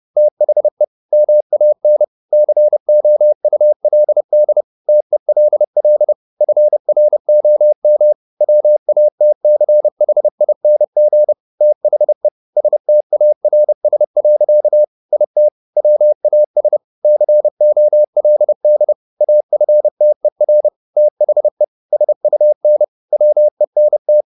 Morse code, heard as a keyed tone, about 30 wpm, 600 Hz. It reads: THE MAN COULD TELL FROM WATCHING THE STARS. IT WAS COLD AFTER THE SUN WENT
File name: the old man and the sea / the old man and the sea 1 30wpm